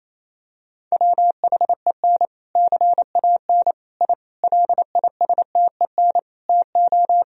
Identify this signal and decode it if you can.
Morse code, heard as a keyed tone, about 28 wpm, 700 Hz. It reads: WHEN CAN I LISTEN TO
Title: Sentences from Top 1000 Words - Review Part 3 - 28wpm